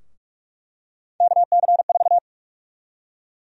Morse code, heard as a keyed tone, about 46 wpm, 700 Hz. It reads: KC4
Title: USA Call Sign Prefixes - Repeated 2x - 45wpm